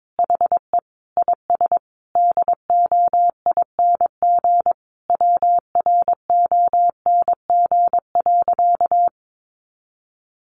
Morse code, heard as a keyed tone, about 22 wpm, 700 Hz. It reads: HE IS DOING WRONG.
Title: Easy Phrases 22wpm